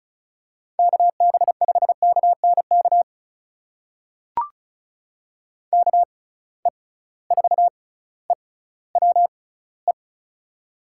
Morse code, heard as a keyed tone, about 36 wpm, spaced as for 18 wpm, 690 Hz. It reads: KB5KNK KE4EWE
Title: USA Call Signs - 6 Characters - 6x Character Spacing - 35wpm